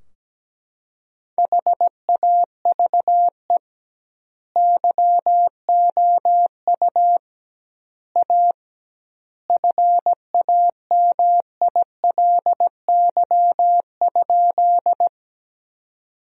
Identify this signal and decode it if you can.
Morse code, heard as a keyed tone, about 17 wpm, 700 Hz. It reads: HAVE YOU A FAMILY?
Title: 2x Word-Spacing - Sentences from Top 400 Words - 17wpm